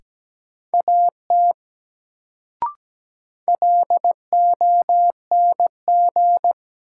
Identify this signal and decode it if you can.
Morse code, heard as a keyed tone, about 17 wpm, 700 Hz. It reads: AT LONG
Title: Sentences from Top 100 Words - New Word Drill - 17wpm